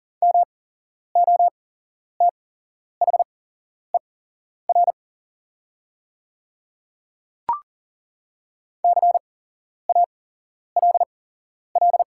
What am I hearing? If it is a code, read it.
Morse code, heard as a keyed tone, about 40 wpm, spaced as for 11 wpm, 690 Hz. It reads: MOTHER CALL
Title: 8x Character Spacing - Top 300 Words - 40wpm